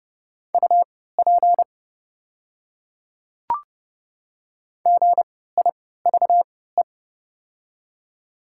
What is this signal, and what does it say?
Morse code, heard as a keyed tone, about 30 wpm, spaced as for 17 wpm, 700 Hz. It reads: UP GIVE
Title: 3x Character Spacing - Top 200 Words - 30wpm